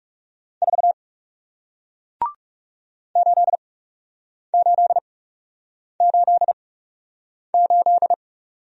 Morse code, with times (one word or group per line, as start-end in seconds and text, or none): 0.6-0.9 4
3.2-8.1 8888